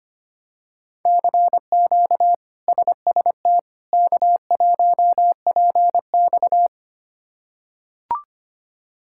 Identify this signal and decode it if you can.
Morse code, heard as a keyed tone, about 25 wpm, 700 Hz. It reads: CQ SST K1PX
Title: K1USN SST QSO Practice - 25wpm